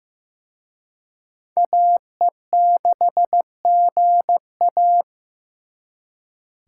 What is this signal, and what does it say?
Morse code, heard as a keyed tone, about 18 wpm, 700 Hz. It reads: AE6GA
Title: Speed Racing - USA 2x2 Call Signs 15wpm